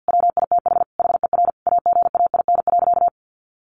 Morse code, heard as a keyed tone, about 50 wpm, 700 Hz. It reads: WITH HER AGAIN.
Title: Sentences from Top 1500 Words 50wpm